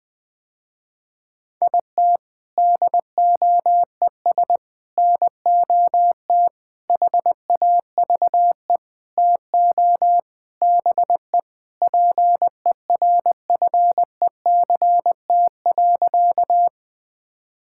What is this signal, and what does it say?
Morse code, heard as a keyed tone, about 21 wpm, 700 Hz. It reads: IT DOES NOT HAVE TO BE PERFECT.
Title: Sentences with 5-Word Phrases - Spoken Beginning - 20wpm